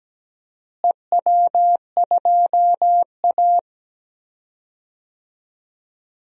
Morse code, heard as a keyed tone, about 17 wpm, 700 Hz. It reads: EW2A
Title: DXCC-100 Call Signs - 4 Characters - Repeated 2x - 17wpm